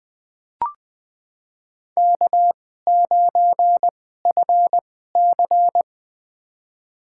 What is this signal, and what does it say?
Morse code, heard as a keyed tone, about 20 wpm, spaced as for 15 wpm, 700 Hz. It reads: K9FC
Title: USA Call Signs - 2x Character Spacing - 20wpm